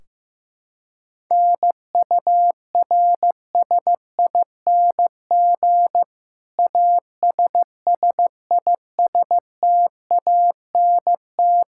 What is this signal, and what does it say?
Morse code, heard as a keyed tone, about 15 wpm, 700 Hz. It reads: NURSING ASSISTANT
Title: Practice QSO - Exchange 3 - More Info - 15wpm